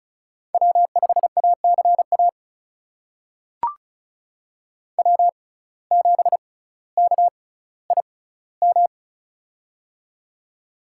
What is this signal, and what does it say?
Morse code, heard as a keyed tone, about 34 wpm, spaced as for 18 wpm, 700 Hz. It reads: W5ACA W7KIM
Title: USA Call Signs - 5 Characters - 6x Character Spacing - 35wpm